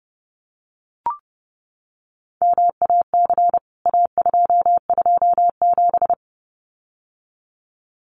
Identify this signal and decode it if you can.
Morse code, heard as a keyed tone, about 30 wpm, 700 Hz. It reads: MAC A227
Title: CWT Exchange 30wpm